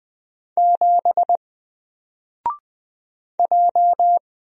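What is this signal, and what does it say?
Morse code, heard as a keyed tone, about 20 wpm, 700 Hz. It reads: 7 J
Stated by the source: Single letter and number Morse drill at 20 wpm, from N4LCD.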